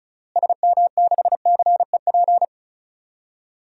Morse code, heard as a keyed tone, about 34 wpm, 700 Hz. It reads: SM6CEP
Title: DXCC-100 Call Signs - 6 Characters - Repeated 3x - 35wpm